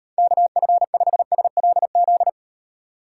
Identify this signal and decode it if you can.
Morse code, heard as a keyed tone, about 38 wpm, 700 Hz. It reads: KF5SLZ